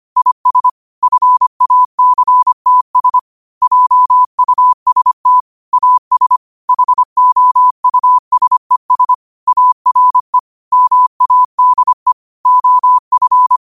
Morse code, heard as a keyed tone, about 25 wpm, 1000 Hz. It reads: IS FACTS JUST AS HOUSES ARE MADE OF
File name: qotdfiles / 25wpm